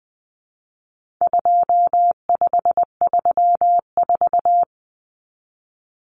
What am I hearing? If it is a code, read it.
Morse code, heard as a keyed tone, about 20 wpm, 700 Hz. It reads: 2534
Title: Numbers - 4-Digit 20wpm